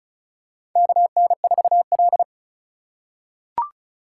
Morse code, heard as a keyed tone, about 34 wpm, 700 Hz. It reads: KN4L
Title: USA Call Signs - 4x Character Spacing - 35wpm